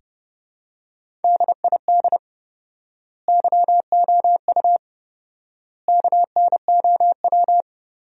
Morse code, heard as a keyed tone, about 30 wpm, 700 Hz. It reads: DID YOU KNOW
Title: 4x Word-Spacing - Sentences from Top 100 Words - 30wpm